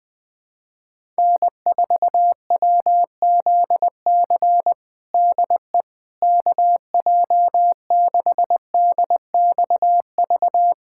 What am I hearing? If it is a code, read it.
Morse code, heard as a keyed tone, about 20 wpm, 700 Hz. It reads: N4WZC DE KJ6DXV
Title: USA Callsign DE Callsign 20wpm